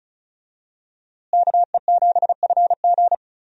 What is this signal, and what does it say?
Morse code, heard as a keyed tone, about 36 wpm, 700 Hz. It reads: KE7FG
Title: USA Call Signs - 5 Characters - Repeated 2x - 35wpm